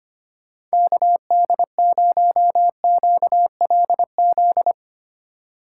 Morse code, heard as a keyed tone, about 32 wpm, 700 Hz. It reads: KD0QLZ